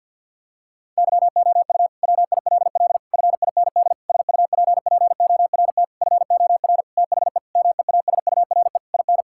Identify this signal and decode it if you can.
Morse code, heard as a keyed tone, about 51 wpm, 700 Hz. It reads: YOU WILL FIND SUPPORT FOR THE MEASURE IN
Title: Sentences from Top 1000 Words - Review Part 5 - 50wpm